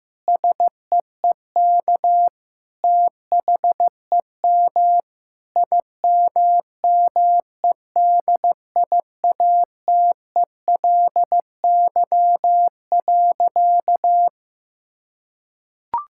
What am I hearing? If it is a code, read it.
Morse code, heard as a keyed tone, about 15 wpm, 700 Hz. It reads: SEEK THEM IMMEDIATELY.
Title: Sentences from Top 1600 Words 15wpm